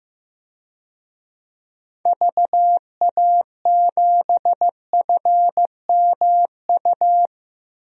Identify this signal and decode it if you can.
Morse code, heard as a keyed tone, about 15 wpm, 700 Hz. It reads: VA7FMU
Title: DXCC-100 Call Signs - 6 Characters - Repeated 2x - 15wpm